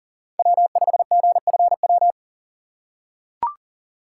Morse code, heard as a keyed tone, about 39 wpm, 700 Hz. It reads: W5GFW
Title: USA Call Signs - 5 Characters - 2x Character Spacing - 40wpm